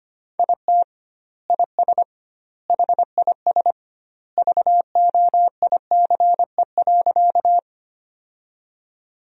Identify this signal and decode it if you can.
Morse code, heard as a keyed tone, about 25 wpm, 700 Hz. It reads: IT IS HIS VOICE.
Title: 2x Word-Spacing - Sentences from Top 500 Words - 25wpm